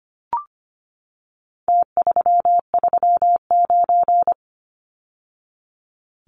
Morse code, heard as a keyed tone, about 25 wpm, 700 Hz. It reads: T339
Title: Numbers - Contest Style - 4-Digit 25wpm